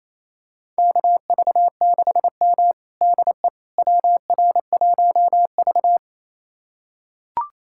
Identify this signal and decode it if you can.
Morse code, heard as a keyed tone, about 28 wpm, 700 Hz. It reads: KV6M DE WR1V
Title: USA Callsign DE Callsign - 4 Characters - 28wpm